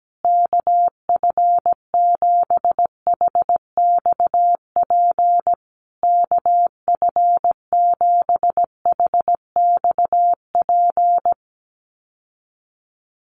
Morse code, heard as a keyed tone, about 17 wpm, 700 Hz. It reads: KF7HXP KF7HXP
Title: USA Call Signs Repeated 3x 17wpm